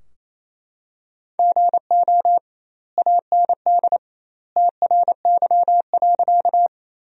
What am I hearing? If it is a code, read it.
Morse code, heard as a keyed tone, about 28 wpm, 700 Hz. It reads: GO AND TRY.
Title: 2x Word-Spacing - Sentences from Top 200 Words - 28wpm